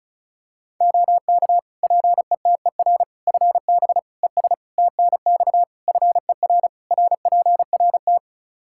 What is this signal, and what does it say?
Morse code, heard as a keyed tone, about 34 wpm, 700 Hz. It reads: OK PETER FB ES TNX FER RPRT
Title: K1USN SST QSO Practice 35wpm